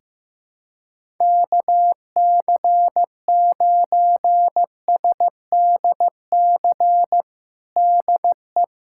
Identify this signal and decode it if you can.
Morse code, heard as a keyed tone, about 15 wpm, 700 Hz. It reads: KC9SDC DE
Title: USA Callsign DE Callsign - 6 Characters - 15wpm